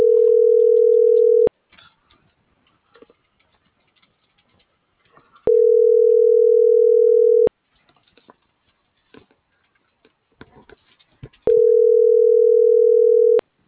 Ringback tone (North America), 440 + 480 Hz.
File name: CENTRAL GENERAL / out yarit.rdz - .90547